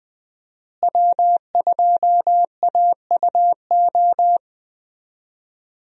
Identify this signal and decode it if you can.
Morse code, heard as a keyed tone, about 20 wpm, 700 Hz. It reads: W2AUO
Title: USA Call Signs - 5 Characters - Repeated 3x - 20wpm